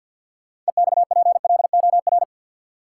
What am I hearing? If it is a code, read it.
Morse code, heard as a keyed tone, about 53 wpm, 700 Hz. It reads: EXPLOR
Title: Sentences from Top 1100 Words - New Word Drill Alt - 50wpm